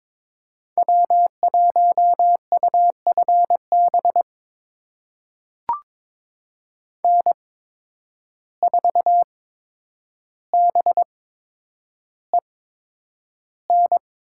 Morse code, heard as a keyed tone, about 22 wpm, spaced as for 9 wpm, 700 Hz. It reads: W1UFB N4BEN
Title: USA Call Signs - 5 Characters - 8x Character Spacing - 22wpm